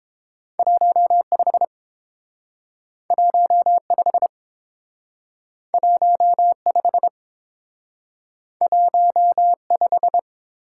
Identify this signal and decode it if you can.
Morse code, heard as a keyed tone, about 30 wpm, 700 Hz. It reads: 15 15 15 15